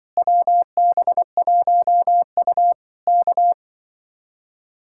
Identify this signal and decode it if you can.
Morse code, heard as a keyed tone, about 24 wpm, 700 Hz. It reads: WB1U K
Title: Speed-Racing Common QSO Phrases - 20wpm